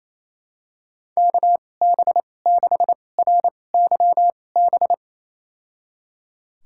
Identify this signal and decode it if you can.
Morse code, heard as a keyed tone, about 28 wpm, spaced as for 20 wpm, 700 Hz. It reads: KB6RYB
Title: USA Call Signs - 2x Character Spacing - 28wpm